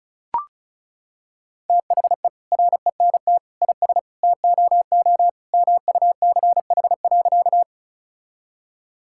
Morse code, heard as a keyed tone, about 36 wpm, 700 Hz. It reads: THE RENT IS TOO MUCH.
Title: Sentences from Top 1000 Words - Review Part 5 - 35wpm